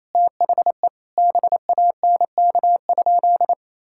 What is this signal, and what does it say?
Morse code, heard as a keyed tone, about 28 wpm, 700 Hz. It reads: THE BANK?
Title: Sentences from Top 1000 Words - Review Part 6 - 28wpm